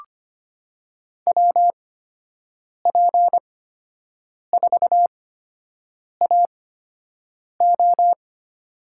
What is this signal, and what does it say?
Morse code, heard as a keyed tone, about 25 wpm, spaced as for 7 wpm, 700 Hz. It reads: WP4AO
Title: USA Call Signs - 8x Character Spacing - 25wpm